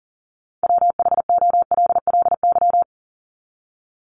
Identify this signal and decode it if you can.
Morse code, heard as a keyed tone, about 40 wpm, 700 Hz. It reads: WHOLLY
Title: LY Suffix - Top 100 Words with Base Words 40wpm